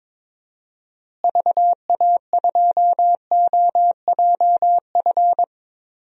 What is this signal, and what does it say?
Morse code, heard as a keyed tone, about 22 wpm, 700 Hz. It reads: VA2OJF
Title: DXCC-100 Call Signs - 6 Characters - Repeated 3x - 22wpm